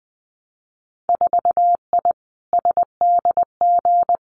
Morse code, heard as a keyed tone, about 21 wpm, 700 Hz. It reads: 4I SDG
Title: Winter Field Day Exchanges Encore 20wpm